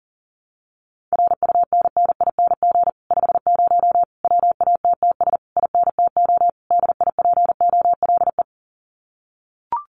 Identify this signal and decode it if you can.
Morse code, heard as a keyed tone, about 40 wpm, 700 Hz. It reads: RUNNING 50 WATTS INTO DIPOLE
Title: Common QSO Phrases Encore 40wpm